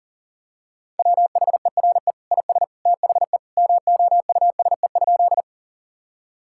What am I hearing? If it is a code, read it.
Morse code, heard as a keyed tone, about 40 wpm, 700 Hz. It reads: WHERE IS THE MOUSE?
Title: Sentences from Top 1000 Words - Review Part 5 - 40wpm